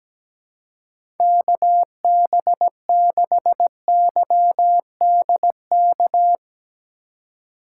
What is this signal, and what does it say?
Morse code, heard as a keyed tone, about 17 wpm, 700 Hz. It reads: KB6YDK
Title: USA Call Signs - 6 Characters - 2x Character Spacing - 17wpm